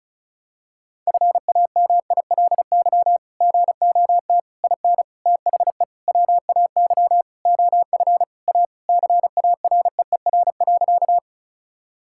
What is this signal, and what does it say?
Morse code, heard as a keyed tone, about 34 wpm, 700 Hz. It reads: FAMILY GOT IN THE WAY OF A CAREER.
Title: Sentences from Top 1000 Words - Review Part 6 - 35wpm